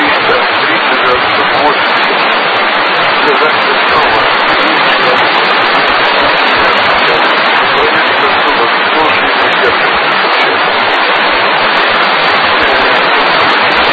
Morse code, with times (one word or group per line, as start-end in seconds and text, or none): none